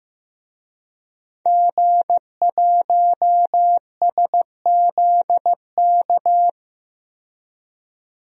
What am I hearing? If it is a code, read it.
Morse code, heard as a keyed tone, about 15 wpm, 700 Hz. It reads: G1SZK
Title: DXCC-100 Call Signs - 5 Characters - 15wpm